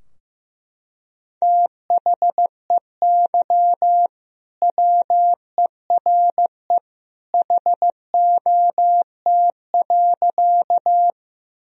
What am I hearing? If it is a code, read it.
Morse code, heard as a keyed tone, about 15 wpm, 700 Hz. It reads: THEY WERE HOT.
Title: Sentences from Top 1000 Words - Review Part 3 - 15wpm